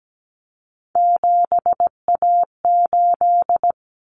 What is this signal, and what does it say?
Morse code, heard as a keyed tone, about 17 wpm, 700 Hz. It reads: 7A8
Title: Common Cut Numbers - 3-Digit 17wpm